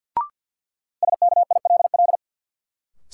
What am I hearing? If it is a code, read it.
Morse code, heard as a keyed tone, about 52 wpm, 700 Hz. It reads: SKILL
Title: Sentences from Top 1200 Words - New Word Drill - 50wpm